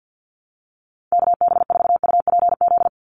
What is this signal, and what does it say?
Morse code, heard as a keyed tone, about 48 wpm, 700 Hz. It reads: KB4UPZ
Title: USA Call Signs 50wpm